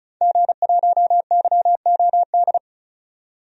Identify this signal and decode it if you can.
Morse code, heard as a keyed tone, about 34 wpm, 700 Hz. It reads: G1YOD
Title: DXCC-100 Call Signs - 5 Characters - 35wpm